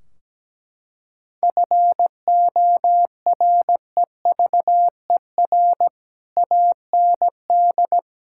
Morse code, heard as a keyed tone, about 17 wpm, 700 Hz. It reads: FOREVER AND
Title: Sentences from Top 1600 Words 17wpm